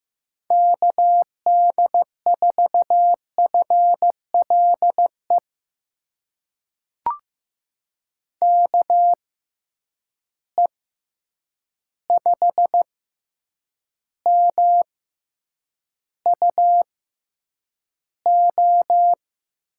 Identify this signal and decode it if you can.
Morse code, heard as a keyed tone, about 15 wpm, spaced as for 8 wpm, 700 Hz. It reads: KD4FLE KE5MUO
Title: USA Call Signs - 6 Characters - 6x Character Spacing - 15wpm